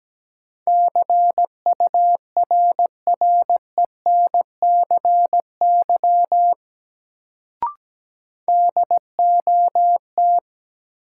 Morse code, heard as a keyed tone, about 17 wpm, 700 Hz. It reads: CURRENCY DOT
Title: Sentences from Top 1500 Words - New Word Drill - 17wpm